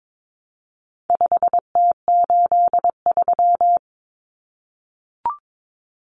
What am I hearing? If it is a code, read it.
Morse code, heard as a keyed tone, about 22 wpm, 700 Hz. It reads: 5T83
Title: Common Cut Numbers - 4-Digit 22wpm